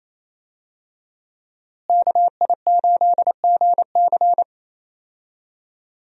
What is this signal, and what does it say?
Morse code, heard as a keyed tone, about 28 wpm, 700 Hz. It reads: KI8GC
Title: USA Call Signs - 5 Characters - Repeated 3x - 28wpm